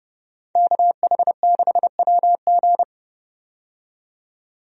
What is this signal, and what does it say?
Morse code, heard as a keyed tone, about 30 wpm, 700 Hz. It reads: KH6WG
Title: DXCC-100 Call Signs - 5 Characters - 30wpm